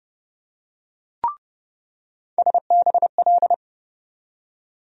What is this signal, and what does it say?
Morse code, heard as a keyed tone, about 30 wpm, 700 Hz. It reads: SBL
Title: USA Call Sign Postfixes - 3 Letters 30wpm